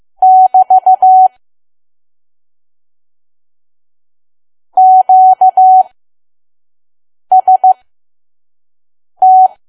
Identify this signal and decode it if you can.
Morse code, heard as a keyed tone, about 15 wpm, spaced as for 5 wpm, 750 Hz. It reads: <BT> QST
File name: Archive / 5 WPM files / PM